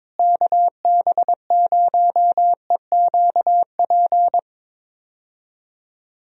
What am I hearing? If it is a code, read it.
Morse code, heard as a keyed tone, about 22 wpm, 700 Hz. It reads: KB0EQP
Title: USA Call Signs - 6 Characters - Repeated 2x - 22wpm